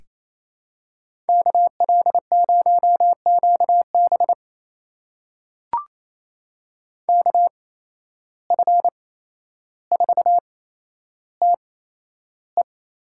Morse code, heard as a keyed tone, about 28 wpm, spaced as for 12 wpm, 700 Hz. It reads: KL0QB KF4TE
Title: USA Call Signs - 5 Characters - 8x Character Spacing - 28wpm